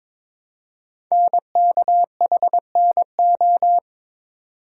Morse code, heard as a keyed tone, about 22 wpm, 700 Hz. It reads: NKHNO
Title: Code Groups - 5 Letters - 22wpm